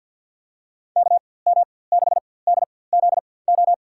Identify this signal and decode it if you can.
Morse code, heard as a keyed tone, about 49 wpm, spaced as for 23 wpm, 700 Hz. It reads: KM6DZO
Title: USA Call Signs - 6 Characters - 4x Character Spacing - 50wpm